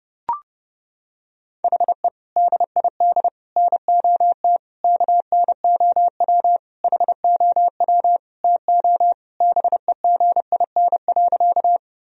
Morse code, heard as a keyed tone, about 30 wpm, 700 Hz. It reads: HE DID NOT KNOW HOW TO BEGIN.
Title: Sentences from Top 1000 Words - Review Part 1 - 30wpm